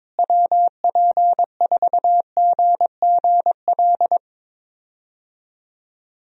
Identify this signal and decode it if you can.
Morse code, heard as a keyed tone, about 21 wpm, 700 Hz. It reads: WP4GGL
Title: USA Call Signs - 6 Characters - Repeated 3x - 22wpm